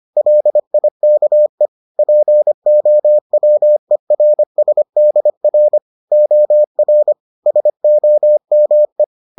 Morse code, heard as a keyed tone, about 25 wpm, 600 Hz. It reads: LIKE POWERSDR OR SOME